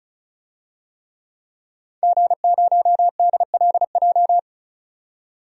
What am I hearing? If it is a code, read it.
Morse code, heard as a keyed tone, about 34 wpm, 700 Hz. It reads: G0DLJ
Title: DXCC-100 Call Signs - 5 Characters - 35wpm